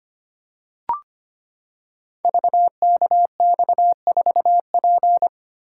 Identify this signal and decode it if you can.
Morse code, heard as a keyed tone, about 25 wpm, 700 Hz. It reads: VKX4P
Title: Code Groups - 5 Letter:Numbers - 25wpm